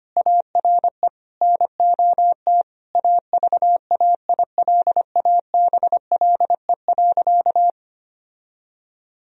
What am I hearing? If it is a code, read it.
Morse code, heard as a keyed tone, about 25 wpm, 700 Hz. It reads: ARE NOT AVAILABLE.
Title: Sentences from Top 1000 Words - Review Part 3 - 25wpm